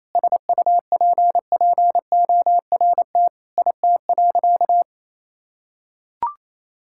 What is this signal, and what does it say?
Morse code, heard as a keyed tone, about 28 wpm, 700 Hz. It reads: SUPPORT IT.
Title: Sentences from Top 1000 Words - Review Part 5 - 28wpm